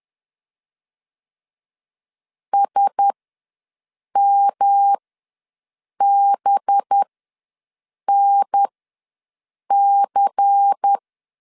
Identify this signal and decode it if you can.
Morse code, heard as a keyed tone, about 10 wpm, spaced as for 6 wpm, 800 Hz. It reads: SMBNC
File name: LEZIONI / 7 GF9Z / 7 GF9 ws 10wpm 4034